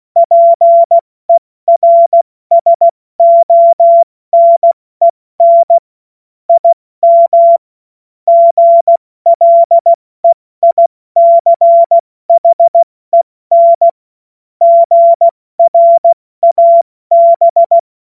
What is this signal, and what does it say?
Morse code, heard as a keyed tone, about 16 wpm, 680 Hz. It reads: PERSONEN IM GLEICHEN GRAB
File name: Morsekurs / Witze / Witz 23. August 2022 70 F80